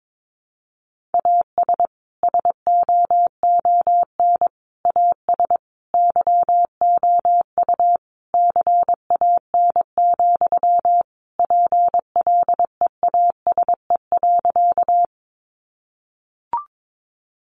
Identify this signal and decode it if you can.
Morse code, heard as a keyed tone, about 22 wpm, 700 Hz. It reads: AS SOON AS YOU CAN, PLEASE.
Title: Sentences from Top 1500 Words 22wpm